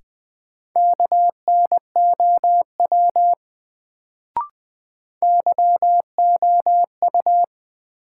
Morse code, heard as a keyed tone, about 20 wpm, 700 Hz. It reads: KNOW YOU
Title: Sets of 2 Words - New Word Drill - 20wpm